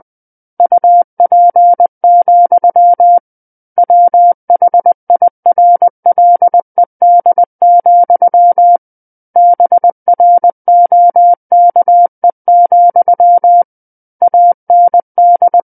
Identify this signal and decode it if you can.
Morse code, heard as a keyed tone, about 20 wpm, 700 Hz. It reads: UP, WHIRLED, BROKE, AND
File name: 11-Chapter 11 - 20 WPM